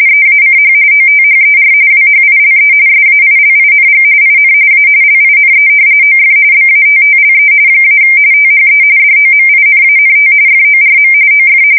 RTTY, about 45.45 baud. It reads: EACH OTHER WHILE SAILING IN A LINE OF BATTLE - THE DOMINANT FORM OF
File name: fsk170 / battleshipnj